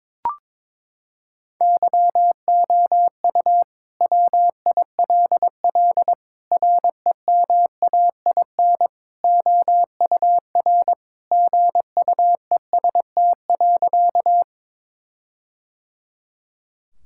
Morse code, heard as a keyed tone, about 22 wpm, 700 Hz. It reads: YOU WILL REMAIN OUR GUEST.
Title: Sentences from Top 1600 Words 22wpm